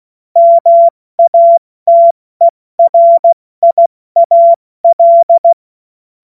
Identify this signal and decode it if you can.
Morse code, heard as a keyed tone, about 16 wpm, 680 Hz. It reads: MATERIAL